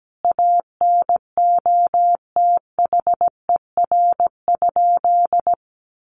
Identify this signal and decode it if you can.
Morse code, heard as a keyed tone, about 17 wpm, 700 Hz. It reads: ANOTHER?
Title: Sentences from Top 1400 Words 17wpm